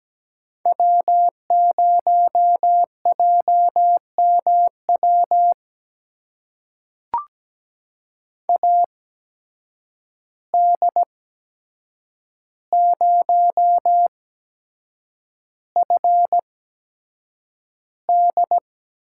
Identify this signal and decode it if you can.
Morse code, heard as a keyed tone, about 17 wpm, spaced as for 7 wpm, 700 Hz. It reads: W0JMW AD0FD
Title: USA Call Signs - 5 Characters - 8x Character Spacing - 17wpm